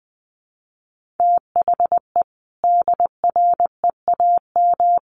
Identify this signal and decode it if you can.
Morse code, heard as a keyed tone, about 20 wpm, 700 Hz. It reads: THE DREAM
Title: Sentences from Top 1500 Words 20wpm